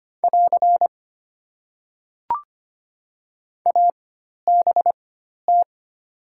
Morse code, heard as a keyed tone, about 25 wpm, spaced as for 12 wpm, 700 Hz. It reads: <AR> ABT
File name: Top 90 QSO Elements - 4x Character Spacing - 25wpm